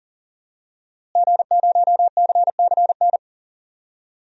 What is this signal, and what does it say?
Morse code, heard as a keyed tone, about 40 wpm, 700 Hz. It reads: G0CCN
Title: DXCC-100 Call Signs - 5 Characters - Repeated 3x - 40wpm